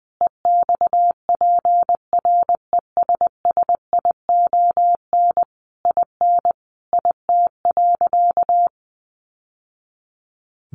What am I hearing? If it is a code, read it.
Morse code, heard as a keyed tone, about 20 wpm, 700 Hz. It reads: EXPRESSION IN IT.
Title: Sentences from Top 1200 Words 20wpm